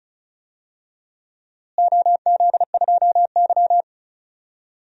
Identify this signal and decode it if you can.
Morse code, heard as a keyed tone, about 36 wpm, 700 Hz. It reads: OZ2Y
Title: DXCC-100 Call Signs - 4 Characters - Repeated 2x - 35wpm